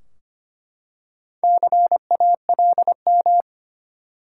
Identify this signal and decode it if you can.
Morse code, heard as a keyed tone, about 25 wpm, 700 Hz. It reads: CALM
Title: Four Letter Words - 25wpm